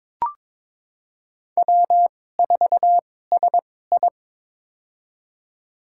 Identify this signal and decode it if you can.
Morse code, heard as a keyed tone, about 22 wpm, spaced as for 16 wpm, 700 Hz. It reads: W4SI
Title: USA Call Signs - 4 Characters - 2x Character Spacing - 22wpm